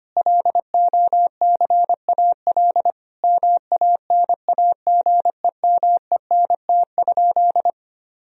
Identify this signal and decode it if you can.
Morse code, heard as a keyed tone, about 25 wpm, 700 Hz. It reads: LOCAL MANAGEMENT?
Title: Sentences from Top 1000 Words - Review Part 4 - 25wpm